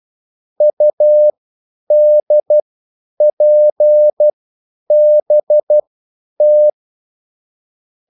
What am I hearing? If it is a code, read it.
Morse code, heard as a keyed tone, about 12 wpm, spaced as for 9 wpm, 600 Hz. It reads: UDPBT